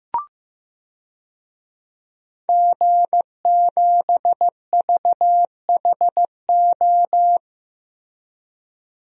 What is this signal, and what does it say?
Morse code, heard as a keyed tone, about 15 wpm, 700 Hz. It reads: G7VHO
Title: DXCC-100 Call Signs - 5 Characters - Repeated 2x - 15wpm